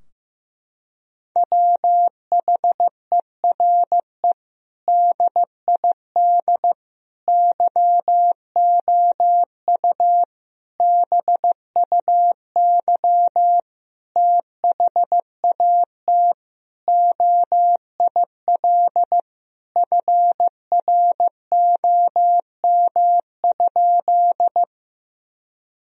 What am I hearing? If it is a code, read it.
Morse code, heard as a keyed tone, about 15 wpm, 700 Hz. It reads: WHERE DID YOU BUY THAT OIL FROM?
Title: Sentences from Top 1000 Words - Review Part 4 - 15wpm